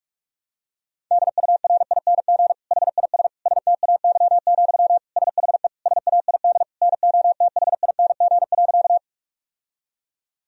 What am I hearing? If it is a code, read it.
Morse code, heard as a keyed tone, about 47 wpm, 700 Hz. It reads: DURING HIS STAY, SHE SAID NOTHING.
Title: Sentences from Top 1000 Words - Review Part 2 - 45wpm